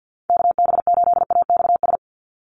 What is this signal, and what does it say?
Morse code, heard as a keyed tone, about 50 wpm, 700 Hz. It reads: KB8AXS
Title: USA Call Signs 50wpm